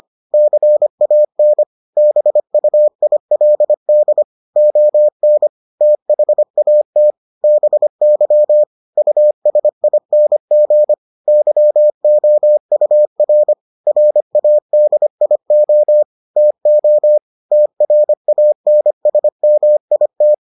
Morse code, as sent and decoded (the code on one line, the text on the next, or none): -.-. .- -. / -... ..- .. .-.. -.. / --- -. / - .... .- - / -... -.-- / ..- ... .. -. --. / -.-- --- ..- .-. / .-. .- -.. .. --- / - --- / - .-. .- -. ... -- .. -
CAN BUILD ON THAT BY USING YOUR RADIO TO TRANSMIT